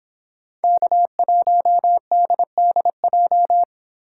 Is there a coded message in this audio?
Yes: Morse code from 0.6 to 3.6 s, heard as a keyed tone, about 26 wpm, 700 Hz. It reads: K1DDJ